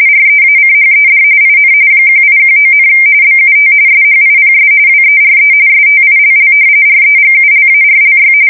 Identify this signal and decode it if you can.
RTTY, about 45.45 baud. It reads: REACHED IN 1819, BUT SHIPS WERE STILL REGULARLY
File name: fsk170 / usnavy2